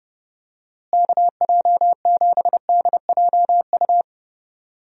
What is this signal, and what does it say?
Morse code, heard as a keyed tone, about 30 wpm, 700 Hz. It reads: KJ7DJU
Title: USA Call Signs - 6 Characters - Repeated 3x - 30wpm